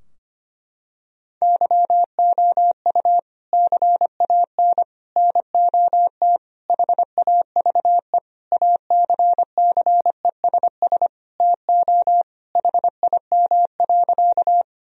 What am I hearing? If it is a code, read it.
Morse code, heard as a keyed tone, about 25 wpm, 700 Hz. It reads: YOU CAN NOT HAVE ACCESS TO HIM.
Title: Sentences from Top 1000 Words - Review Part 4 - 25wpm